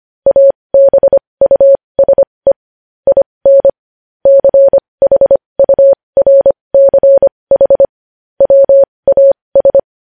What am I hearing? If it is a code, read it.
Morse code, heard as a keyed tone, about 25 wpm, spaced as for 20 wpm, 550 Hz. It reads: ABUSE IN CHURCH WAS
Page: news-headlines-in-morse-code-at-20-wpm